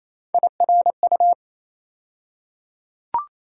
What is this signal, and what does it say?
Morse code, heard as a keyed tone, about 28 wpm, 700 Hz. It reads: IRU
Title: USA Call Sign Postfixes - 3 Letters 28wpm